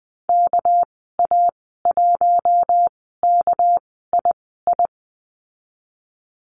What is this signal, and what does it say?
Morse code, heard as a keyed tone, about 20 wpm, spaced as for 15 wpm, 700 Hz. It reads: KA1KII
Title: 231 Lesson 37g - Letter x - Call Signs 20wpm